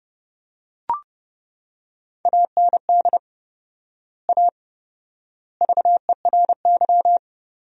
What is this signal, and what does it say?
Morse code, heard as a keyed tone, about 31 wpm, 700 Hz. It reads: AND A VERY
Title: 4x Word-Spacing - Sentences from Top 200 Words - 30wpm